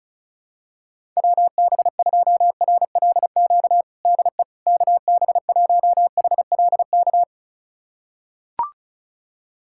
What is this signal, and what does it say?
Morse code, heard as a keyed tone, about 36 wpm, 700 Hz. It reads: WB2RLQ DE KB1HLK
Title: USA Callsign DE Callsign - 6 Characters - 35wpm